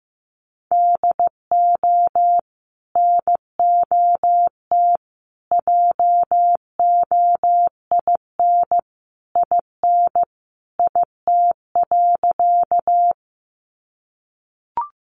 Morse code, heard as a keyed tone, about 15 wpm, 700 Hz. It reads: DO NOT JOIN IN IT.
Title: Sentences from Top 1400 Words 15wpm